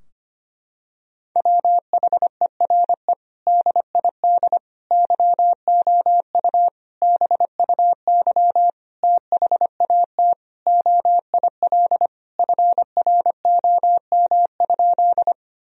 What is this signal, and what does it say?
Morse code, heard as a keyed tone, about 25 wpm, 700 Hz. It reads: WHERE DID YOU BUY THAT OIL FROM?
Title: Sentences from Top 1000 Words - Review Part 4 - 25wpm